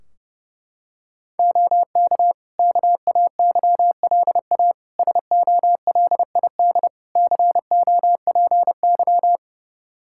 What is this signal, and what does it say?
Morse code, heard as a keyed tone, about 30 wpm, 700 Hz. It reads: OK KAYLA SOLID COPY
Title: K1USN SST QSO Practice 30wpm